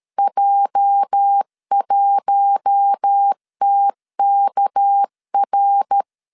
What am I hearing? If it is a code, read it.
Morse code, heard as a keyed tone, about 13 wpm, 800 Hz. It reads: J1TKR
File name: LEZIONI / 8 YJ45 / 8 YJ 12wpm 2235